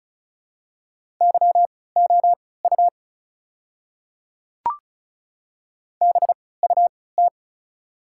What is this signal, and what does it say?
Morse code, heard as a keyed tone, about 36 wpm, spaced as for 20 wpm, 690 Hz. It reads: YOU BUT
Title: 3x Character Spacing - Top 100 Words - 35wpm